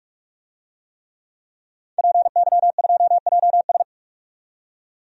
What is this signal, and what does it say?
Morse code, heard as a keyed tone, about 46 wpm, 700 Hz. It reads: PY2JS
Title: DXCC-100 Call Signs - 5 Characters - 45wpm